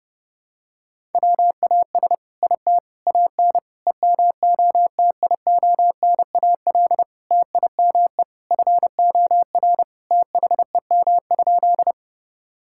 Morse code, heard as a keyed tone, about 32 wpm, 700 Hz. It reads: WAS IT AN EMOTIONAL TIME FOR THEM?
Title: Sentences from Top 1000 Words - Review Part 5 - 30wpm